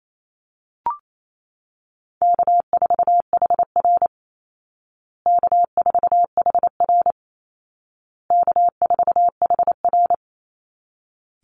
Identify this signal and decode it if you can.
Morse code, heard as a keyed tone, about 28 wpm, 700 Hz. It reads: K4HR K4HR K4HR
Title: Call Signs Repeated 2x - 28wpm